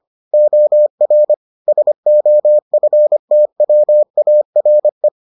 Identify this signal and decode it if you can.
Morse code, heard as a keyed tone, about 25 wpm, 600 Hz. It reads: OR SOFTWARE